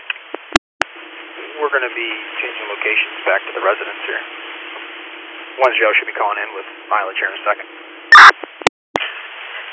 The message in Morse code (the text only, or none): none